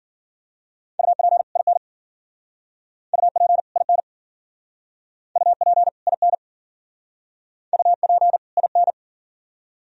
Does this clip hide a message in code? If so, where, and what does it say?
Yes: Morse code from 1.0 to 8.9 s, heard as a keyed tone, about 51 wpm, 690 Hz. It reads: UP IN UP IN UP IN UP IN